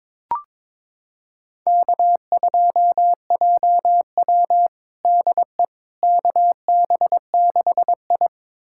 Morse code, heard as a keyed tone, about 23 wpm, 700 Hz. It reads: K2JW DE KB6I
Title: USA Callsign DE Callsign - 4 Characters - 22wpm